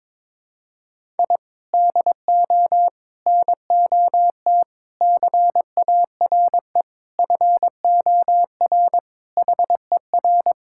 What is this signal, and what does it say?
Morse code, heard as a keyed tone, about 21 wpm, 700 Hz. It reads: I DO NOT CARE FOR HER
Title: Sentences from Top 1000 Words - Review Part 6 - 22wpm